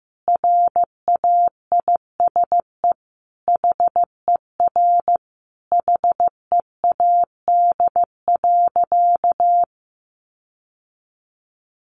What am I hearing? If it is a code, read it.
Morse code, heard as a keyed tone, about 15 wpm, 700 Hz. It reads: RAISE HER HEAD.
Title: Sentences from Top 1400 Words 15wpm